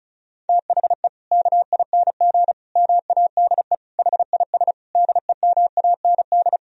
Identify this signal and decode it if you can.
Morse code, heard as a keyed tone, about 34 wpm, 700 Hz. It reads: THE KING MADE HIS DEMAND
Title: Sentences from Top 1000 Words - Review Part 4 - 35wpm